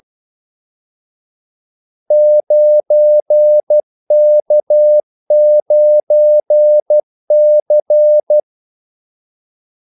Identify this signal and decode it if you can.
Morse code, heard as a keyed tone, about 12 wpm, 600 Hz. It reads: 9K9C
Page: 2 ekstra ordmellemrum mellem kaldesignalerne